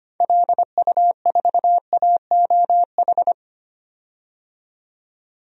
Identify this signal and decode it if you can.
Morse code, heard as a keyed tone, about 25 wpm, 700 Hz. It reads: LU4AOH
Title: DXCC-100 Call Signs - 6 Characters - 25wpm